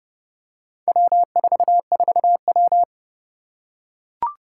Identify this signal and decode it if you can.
Morse code, heard as a keyed tone, about 30 wpm, 700 Hz. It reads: W44W
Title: Four Challenging Letter-Number Combinations 30wpm